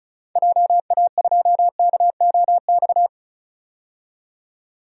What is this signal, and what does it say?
Morse code, heard as a keyed tone, about 36 wpm, 700 Hz. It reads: JA2KOX
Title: DXCC-100 Call Signs - 6 Characters - 35wpm